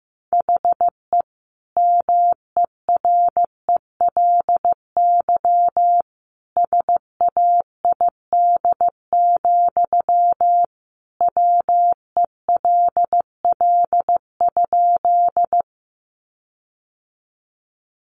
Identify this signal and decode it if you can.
Morse code, heard as a keyed tone, about 15 wpm, 700 Hz. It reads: HE MERELY SAID, WELL?
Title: Sentences from Top 1400 Words 15wpm